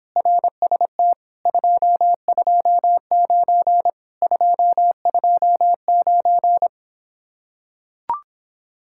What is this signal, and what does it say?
Morse code, heard as a keyed tone, about 26 wpm, 700 Hz. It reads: RST 229 229